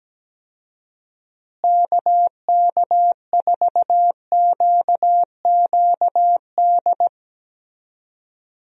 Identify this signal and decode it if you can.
Morse code, heard as a keyed tone, about 17 wpm, 700 Hz. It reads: KK4QQD
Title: USA Call Signs - 6 Characters - Repeated 2x - 17wpm